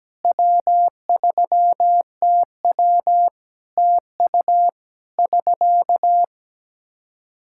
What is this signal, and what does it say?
Morse code, heard as a keyed tone, about 17 wpm, 700 Hz. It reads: W3TW TU <SK>
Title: Practice QSO - Exchange 4 - Ending - 17wpm